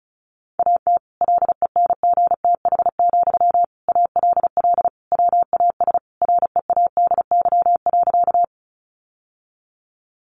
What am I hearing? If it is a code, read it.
Morse code, heard as a keyed tone, about 34 wpm, 700 Hz. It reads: AT LENGTH, ALL WAS READY.
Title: Sentences from Top 1000 Words 35wpm